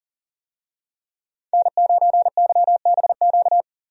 Morse code, heard as a keyed tone, about 40 wpm, 700 Hz. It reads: N9YBQ
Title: USA Call Signs - 5 Characters - Repeated 3x - 40wpm